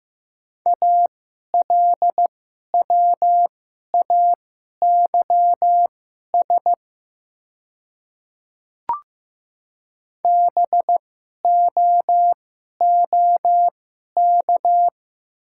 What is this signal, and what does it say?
Morse code, heard as a keyed tone, about 15 wpm, spaced as for 11 wpm, 700 Hz. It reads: ALWAYS BOOK
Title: 2x Character Spacing - Top 300 Words - 15wpm